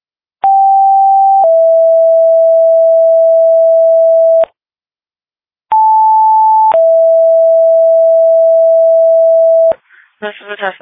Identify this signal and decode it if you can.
Two-tone page, heard as a one-second tone, then a three-second tone, about 780 then 660 Hz, about 880 then 660 Hz, with speech. It is heard from 0.4 to 9.7 s.
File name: dispatch / Frederika Fire 2026 09 03 19 03 39